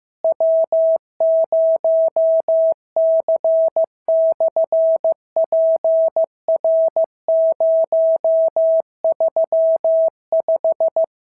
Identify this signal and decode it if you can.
Morse code, heard as a keyed tone, about 15 wpm, 650 Hz. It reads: W0C/PR035
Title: SOTA QSO Practice 15wpm